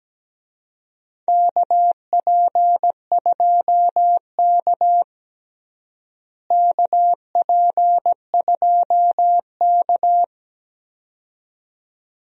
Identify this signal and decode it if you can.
Morse code, heard as a keyed tone, about 17 wpm, 700 Hz. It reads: KP2K KP2K
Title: DXCC-100 Call Signs - 4 Characters - Repeated 2x - 17wpm